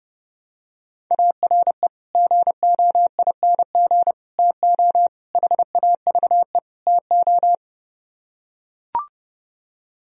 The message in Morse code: ARE GOING TO HAVE TO

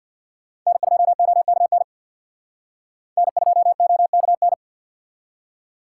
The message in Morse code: N2OKN N2OKN

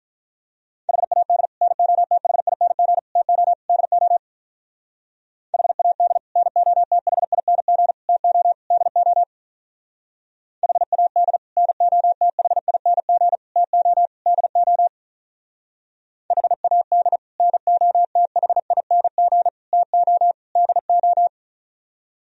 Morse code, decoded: HAD NOTHING TO DO HAD NOTHING TO DO HAD NOTHING TO DO HAD NOTHING TO DO

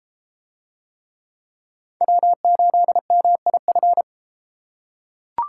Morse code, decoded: W8MIF

